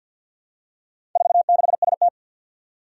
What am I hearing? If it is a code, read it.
Morse code, heard as a keyed tone, about 50 wpm, 700 Hz. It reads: 46ST
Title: Four Challenging Letter-Number Combinations 50wpm